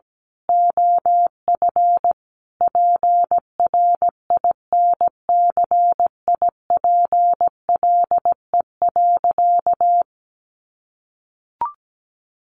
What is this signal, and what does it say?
Morse code, heard as a keyed tone, about 17 wpm, 700 Hz. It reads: OF PRINCIPLE.
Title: Sentences from Top 1400 Words 17wpm